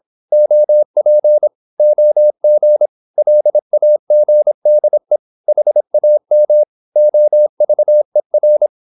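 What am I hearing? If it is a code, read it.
Morse code, heard as a keyed tone, about 26 wpm, 600 Hz. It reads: OP OG LAGDE HAM OVER
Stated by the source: High-speed (QRQ) Morse at 26 wpm.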